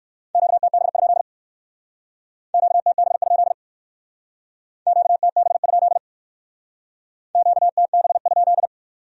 Morse code, heard as a keyed tone, about 68 wpm, 700 Hz. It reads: QTB? QTB? QTB? QTB?